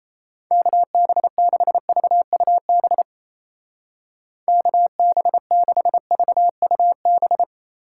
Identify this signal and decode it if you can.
Morse code, heard as a keyed tone, about 36 wpm, 700 Hz. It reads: KB6VUB KB6VUB